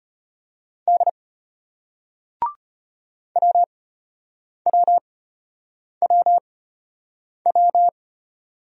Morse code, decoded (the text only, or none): D WWWW